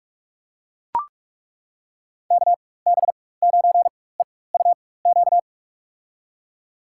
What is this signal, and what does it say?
Morse code, heard as a keyed tone, about 42 wpm, spaced as for 21 wpm, 700 Hz. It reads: KB9EUQ
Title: USA Call Signs - 6 Characters - 4x Character Spacing - 45wpm